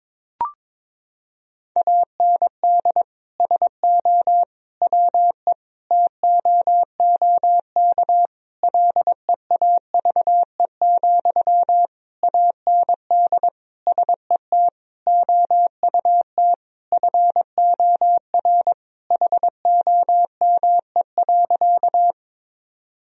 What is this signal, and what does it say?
Morse code, heard as a keyed tone, about 22 wpm, 700 Hz. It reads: AND SO WE TOOK LEAVE, AND SET OUT FOR HOME.
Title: Sentences from Top 1000 Words - Review Part 1 - 22wpm